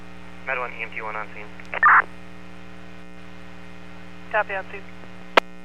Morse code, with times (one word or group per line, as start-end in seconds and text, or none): none